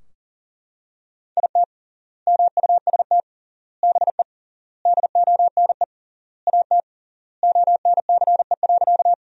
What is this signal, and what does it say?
Morse code, heard as a keyed tone, about 40 wpm, 700 Hz. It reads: IT MUST BE DONE AT ONCE.
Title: 3x Word-Spacing - Sentences from Top 500 Words - 40wpm